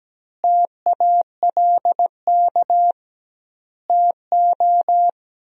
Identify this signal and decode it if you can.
Morse code, heard as a keyed tone, about 17 wpm, 700 Hz. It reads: TALK TO
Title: 2x Word-Spacing - Sentences from Top 400 Words - 17wpm